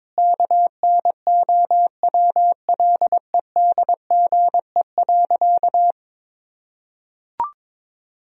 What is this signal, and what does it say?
Morse code, heard as a keyed tone, about 22 wpm, 700 Hz. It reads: KNOWLEDGE.
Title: Sentences from Top 1000 Words - Review Part 3 - 22wpm